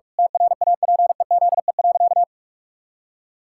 Morse code, heard as a keyed tone, about 46 wpm, 700 Hz. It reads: TRAPEZE.
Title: Pangram Challenge v2 - 45wpm